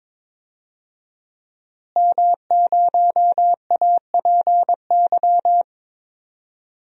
Morse code, heard as a keyed tone, about 22 wpm, 700 Hz. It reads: M0APY
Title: DXCC-100 Call Signs - 5 Characters - Repeated 2x - 22wpm